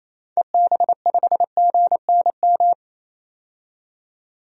Morse code, heard as a keyed tone, about 28 wpm, 700 Hz. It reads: EB5GNM
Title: DXCC-100 Call Signs - 6 Characters - 28wpm